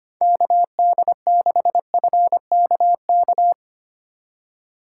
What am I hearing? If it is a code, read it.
Morse code, heard as a keyed tone, about 31 wpm, 700 Hz. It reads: KD6FKK
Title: Speed Racing - USA 2x3 Call Signs 25wpm